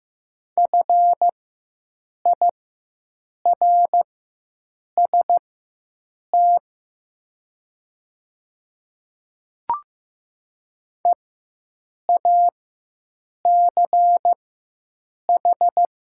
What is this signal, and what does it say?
Morse code, heard as a keyed tone, about 15 wpm, spaced as for 7 wpm, 700 Hz. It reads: FIRST EACH
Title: 4x Character Spacing - Top 100 Words - 15wpm